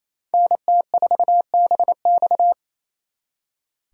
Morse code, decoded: NT4BX

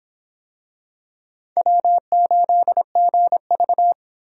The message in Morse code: W8GV

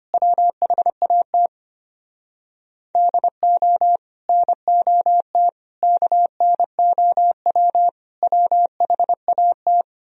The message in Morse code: WHAT DO NOT KNOW WHAT

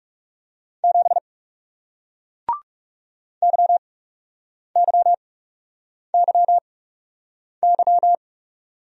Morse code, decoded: 7 YYYY

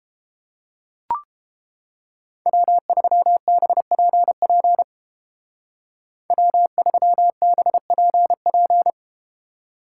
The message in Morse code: W3BPP W3BPP

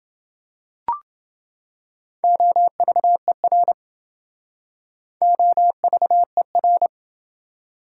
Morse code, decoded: OVER OVER

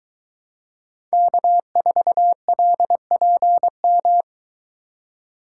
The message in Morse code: K4LPM

